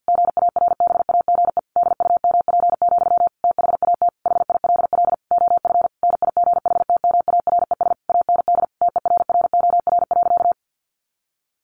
Morse code, heard as a keyed tone, about 50 wpm, 700 Hz. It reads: GARBAGE DUMP, THAT HILL OF NIGHTMARES AND TERROR.
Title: A Sound Of Thunder 50wpm